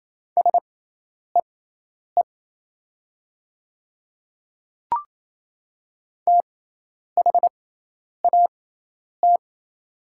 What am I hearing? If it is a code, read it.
Morse code, heard as a keyed tone, about 27 wpm, spaced as for 10 wpm, 700 Hz. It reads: SEE THAT
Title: 6x Character Spacing - Top 100 Words - 28wpm